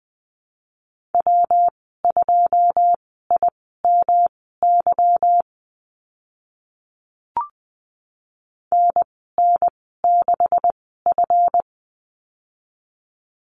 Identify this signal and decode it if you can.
Morse code, heard as a keyed tone, about 20 wpm, spaced as for 15 wpm, 700 Hz. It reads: W2IMY NN6F
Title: 154 Lesson 25g - Letter p - Call Signs 20wpm